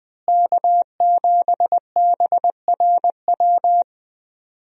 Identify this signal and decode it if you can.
Morse code, heard as a keyed tone, about 20 wpm, 700 Hz. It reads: K7BRW
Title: USA Call Signs - 5 Characters - Repeated 3x - 20wpm